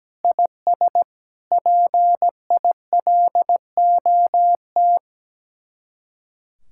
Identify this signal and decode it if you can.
Morse code, heard as a keyed tone, about 17 wpm, 700 Hz. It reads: IS PILOT
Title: Speed-Racing Common QSO Phrases Encore - 17wpm